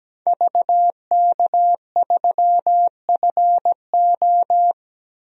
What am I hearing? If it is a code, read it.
Morse code, heard as a keyed tone, about 17 wpm, 700 Hz. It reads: VK3FO
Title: DXCC-100 Call Signs - 5 Characters - 17wpm